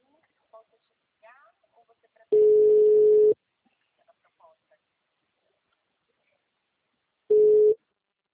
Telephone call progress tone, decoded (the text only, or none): ringback tone